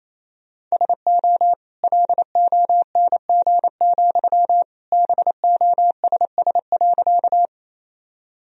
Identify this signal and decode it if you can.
Morse code, heard as a keyed tone, about 28 wpm, 700 Hz. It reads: SO LONG, BOSS.
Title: Sentences from Top 1000 Words - Review Part 4 - 28wpm